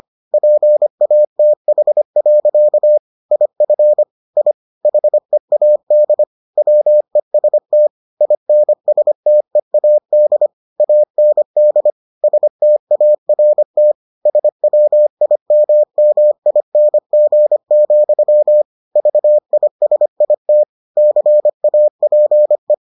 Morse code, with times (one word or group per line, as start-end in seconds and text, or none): 0.3-3.0 PATH.
3.3-4.0 IF
4.4-4.5 I
4.9-6.2 HEAD
6.6-7.9 WEST
8.2-10.5 INSTEAD
10.8-11.9 AND
12.2-13.9 START
14.3-18.6 SWIMMING,
19.0-20.6 VISIT
21.0-22.7 CAPE